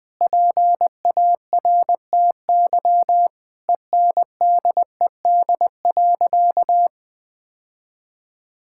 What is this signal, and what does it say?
Morse code, heard as a keyed tone, about 19 wpm, 700 Hz. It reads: PARTY ENDED.
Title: Sentences with 2-Word Phrases - 20wpm